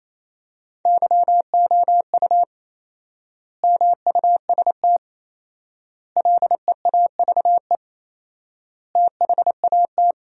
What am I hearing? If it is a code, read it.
Morse code, heard as a keyed tone, about 27 wpm, 700 Hz. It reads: YOU MUST LEAVE THAT
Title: 4x Word-Spacing - Sentences from Top 400 Words - 28wpm